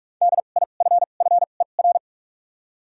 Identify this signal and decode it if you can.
Morse code, heard as a keyed tone, about 46 wpm, spaced as for 32 wpm, 700 Hz. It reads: DIFFER
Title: 2x Character Spacing - Top 500 Words - 45wpm